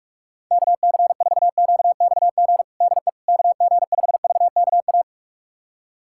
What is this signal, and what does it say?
Morse code, heard as a keyed tone, about 42 wpm, 690 Hz. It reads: KC4QXG DE KG5VKA
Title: USA Callsign DE Callsign - 6 Characters - 45wpm